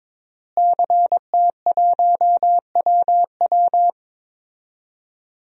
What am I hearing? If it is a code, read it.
Morse code, heard as a keyed tone, about 22 wpm, 700 Hz. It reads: CT1WW
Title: DXCC-100 Call Signs - 5 Characters - Repeated 3x - 22wpm